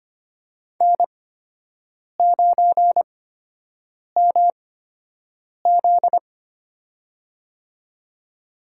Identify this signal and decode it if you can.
Morse code, heard as a keyed tone, about 25 wpm, spaced as for 7 wpm, 700 Hz. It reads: N9MZ
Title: USA Call Signs - 4 Characters - 8x Character Spacing - 25wpm